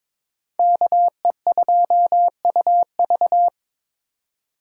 Morse code, heard as a keyed tone, about 22 wpm, 700 Hz. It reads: KE2UV
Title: USA Call Signs - 5 Characters - 3x Character Spacing - 22wpm